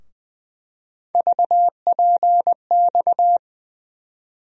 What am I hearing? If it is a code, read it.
Morse code, heard as a keyed tone, about 20 wpm, 700 Hz. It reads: VPX
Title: USA Call Sign Postfixes - 3 Letters 20wpm